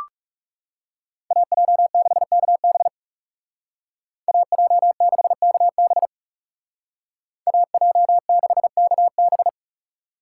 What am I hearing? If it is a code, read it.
Morse code, heard as a keyed tone, about 36 wpm, 700 Hz. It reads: AJ6KB AJ6KB AJ6KB
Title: Speed Racing - USA 2x2 Call Signs 30wpm